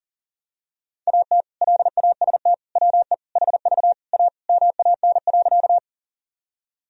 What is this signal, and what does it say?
Morse code, heard as a keyed tone, about 40 wpm, 700 Hz. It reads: AT LAST WE HV A MAN.
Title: Sentences from Top 300 Words and Abbreviations 40wpm